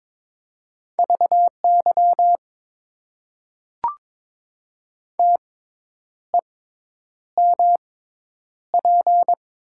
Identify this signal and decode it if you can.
Morse code, heard as a keyed tone, about 22 wpm, spaced as for 9 wpm, 700 Hz. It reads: VY TEMP
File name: Top 90 QSO Elements - 6x Character Spacing - 22wpm